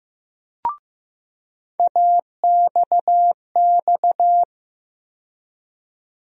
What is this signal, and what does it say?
Morse code, heard as a keyed tone, about 15 wpm, 700 Hz. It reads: AXX
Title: USA Call Sign Postfixes - 3 Letters 15wpm